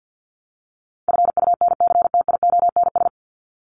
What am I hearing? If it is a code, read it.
Morse code, heard as a keyed tone, about 50 wpm, 700 Hz. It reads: FUNCTIONS
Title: ES:S Suffix - Top 100 Words with Base Words 50wpm